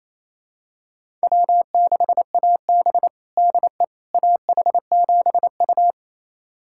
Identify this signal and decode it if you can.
Morse code, heard as a keyed tone, about 27 wpm, 700 Hz. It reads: W6AB DE AH7U
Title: USA Callsign DE Callsign - 4 Characters - 28wpm